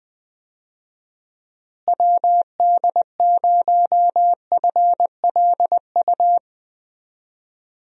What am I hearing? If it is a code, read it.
Morse code, heard as a keyed tone, about 20 wpm, 700 Hz. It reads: WD0FLU
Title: USA Call Signs - 6 Characters - Repeated 3x - 20wpm